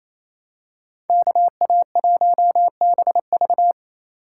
Morse code, heard as a keyed tone, about 28 wpm, 700 Hz. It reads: KA1BV
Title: USA Call Signs - 5 Characters - 4x Character Spacing - 28wpm